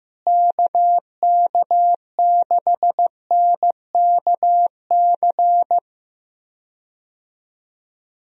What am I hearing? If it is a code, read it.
Morse code, heard as a keyed tone, about 15 wpm, 700 Hz. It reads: KK6NKC
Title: USA Call Signs - 6 Characters - Repeated 3x - 15wpm